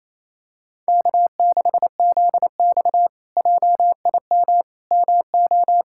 Morse code, heard as a keyed tone, about 27 wpm, 700 Hz. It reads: K6ZX JIM MO
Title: K1USN SST QSO Practice - 28wpm